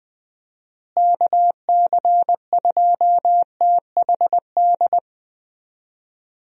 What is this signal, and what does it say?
Morse code, heard as a keyed tone, about 20 wpm, 700 Hz. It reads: KC2THD
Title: USA Call Signs - 6 Characters - Repeated 3x - 20wpm